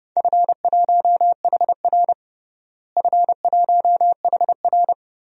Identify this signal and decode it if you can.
Morse code, heard as a keyed tone, about 32 wpm, 700 Hz. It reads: F1HR F1HR
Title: DXCC-100 Call Signs - 4 Characters - Repeated 3x - 30wpm